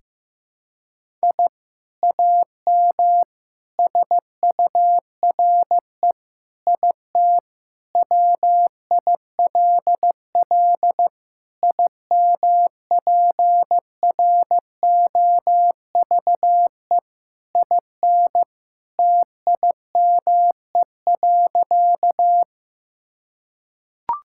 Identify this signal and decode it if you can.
Morse code, heard as a keyed tone, about 15 wpm, 700 Hz. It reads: I AM SURE IT WILL IMPROVE IN TIME.
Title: Sentences from Top 1000 Words - Review Part 5 - 15wpm